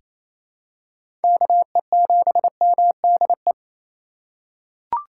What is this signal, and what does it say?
Morse code, heard as a keyed tone, about 28 wpm, 690 Hz. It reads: KE7MDE
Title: USA Call Signs - 2x Character Spacing - 28wpm